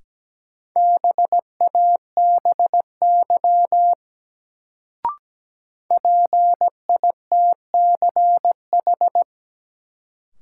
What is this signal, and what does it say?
Morse code, heard as a keyed tone, about 17 wpm, 700 Hz. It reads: BABY PITCH
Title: Sentences from Top 1300 Words - New Word Drill - 17wpm